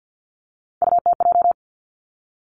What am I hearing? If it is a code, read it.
Morse code, heard as a keyed tone, about 50 wpm, 700 Hz. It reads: UTJ
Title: Three Challenging Letter-Number Combinations 50wpm